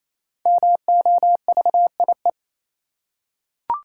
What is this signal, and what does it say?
Morse code, heard as a keyed tone, about 28 wpm, 700 Hz. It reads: MOVIE
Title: Five Letter Words - 28wpm